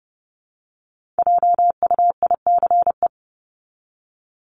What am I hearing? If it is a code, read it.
Morse code, heard as a keyed tone, about 29 wpm, 700 Hz. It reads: JUICE
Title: Top 1251-1500 Words 30wpm